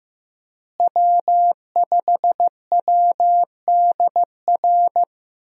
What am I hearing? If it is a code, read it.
Morse code, heard as a keyed tone, about 16 wpm, 700 Hz. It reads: W5WDR
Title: USA Call Signs - 5 Characters - Repeated 2x - 15wpm